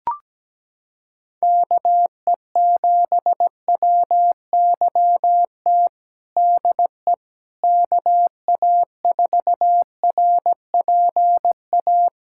Morse code, heard as a keyed tone, about 17 wpm, 700 Hz. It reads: KE7WYT DE KA4RPA
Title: USA Callsign DE Callsign 17wpm